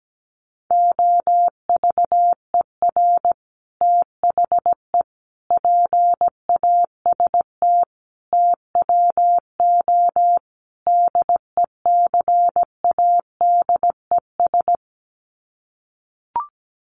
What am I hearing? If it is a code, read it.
Morse code, heard as a keyed tone, about 17 wpm, 700 Hz. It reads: OVER THE PAST TWO DECADES
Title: Sets of 5 Words Encore2 17wpm